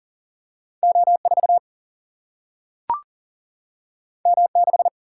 Morse code, heard as a keyed tone, about 38 wpm, 700 Hz. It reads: O4 M6
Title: Single-Letter Single-Number 40wpm